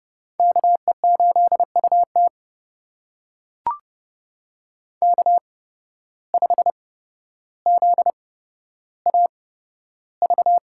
Morse code, heard as a keyed tone, about 30 wpm, spaced as for 13 wpm, 700 Hz. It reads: KE8UT K5ZAV
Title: USA Call Signs - 5 Characters - 8x Character Spacing - 30wpm